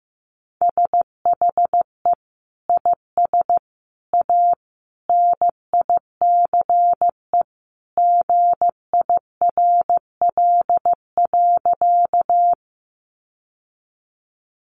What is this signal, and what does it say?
Morse code, heard as a keyed tone, about 15 wpm, 700 Hz. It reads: SHE IS A NICE GIRL.
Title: Sentences from Top 1100 Words 15wpm